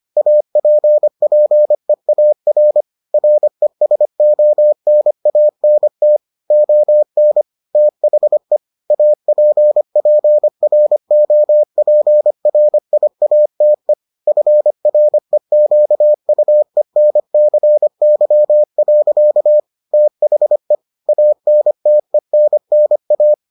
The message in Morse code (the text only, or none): APPEAR RESONANT ON THE APPROPRIATE FREQUENCY. THE ANTENNA